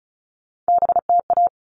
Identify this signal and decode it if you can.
Morse code, heard as a keyed tone, about 34 wpm, 700 Hz. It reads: BTA
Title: All Cut Numbers - 3-Digit 35wpm